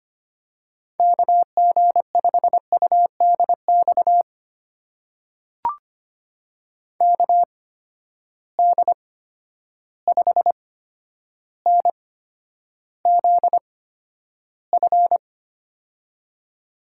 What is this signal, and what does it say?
Morse code, heard as a keyed tone, about 25 wpm, spaced as for 11 wpm, 700 Hz. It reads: KG5UDX KD5NZF
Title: USA Call Signs - 6 Characters - 8x Character Spacing - 25wpm